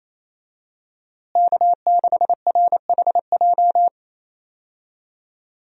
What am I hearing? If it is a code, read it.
Morse code, heard as a keyed tone, about 28 wpm, 700 Hz. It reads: K6RHJ
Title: USA Call Signs - 5 Characters - Repeated 2x - 28wpm